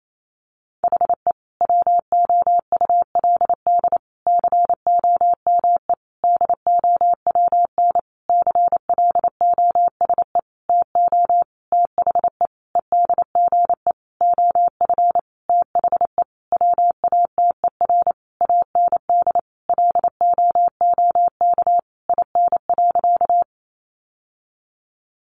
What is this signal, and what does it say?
Morse code, heard as a keyed tone, about 28 wpm, 700 Hz. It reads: HE WOULD COME DOWN CLOSE TO THE EDGE OF THE WATER AND LOOK IN.
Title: Sentences from Top 1300 Words 28wpm